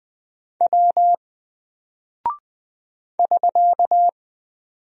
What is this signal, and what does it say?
Morse code, heard as a keyed tone, about 20 wpm, 700 Hz. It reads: W <SK>
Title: Single Character (Letter:Number:Punctuation:Prosign) 20wpm